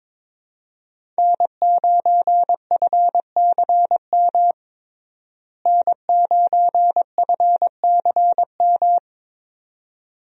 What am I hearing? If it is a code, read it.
Morse code, heard as a keyed tone, about 22 wpm, 700 Hz. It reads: N9FCM N9FCM
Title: USA Call Signs - 5 Characters - Repeated 2x - 22wpm